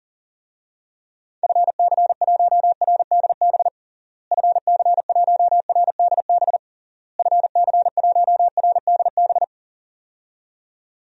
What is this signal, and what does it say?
Morse code, heard as a keyed tone, about 41 wpm, 700 Hz. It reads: FC1RDB FC1RDB FC1RDB
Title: DXCC-100 Call Signs - 6 Characters - Repeated 3x - 40wpm